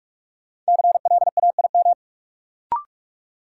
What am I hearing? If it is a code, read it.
Morse code, heard as a keyed tone, about 43 wpm, 700 Hz. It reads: CLAIM
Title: Sentences from Top 1500 Words - New Word Drill Alt - 45wpm